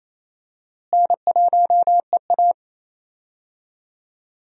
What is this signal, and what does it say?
Morse code, heard as a keyed tone, about 27 wpm, 700 Hz. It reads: N1EA
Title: USA Call Signs - 4 Characters - Repeated 2x - 28wpm